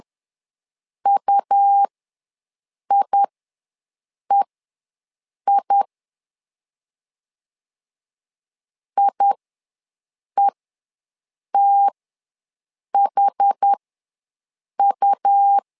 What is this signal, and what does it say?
Morse code, heard as a keyed tone, about 10 wpm, spaced as for 6 wpm, 800 Hz. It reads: UIEI IETHU
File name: LEZIONI / 2 UHDN / 2 UHD ws 10wpm 2058